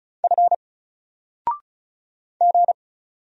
Morse code, heard as a keyed tone, about 35 wpm, 700 Hz. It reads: F G
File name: Single-Letter-Number-35wpm